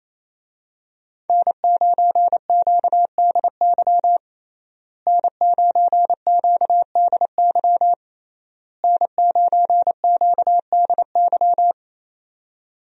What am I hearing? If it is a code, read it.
Morse code, heard as a keyed tone, about 28 wpm, 700 Hz. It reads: N9QDY N9QDY N9QDY
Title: USA Call Signs - 5 Characters - Repeated 3x - 28wpm